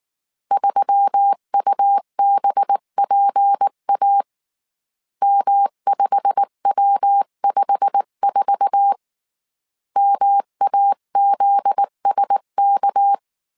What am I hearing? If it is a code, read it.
Morse code, heard as a keyed tone, about 19 wpm, 800 Hz. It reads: 3UBPA M5W54 MAZSK
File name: CW / DET / 1 9 1500C S B 18wpm 2551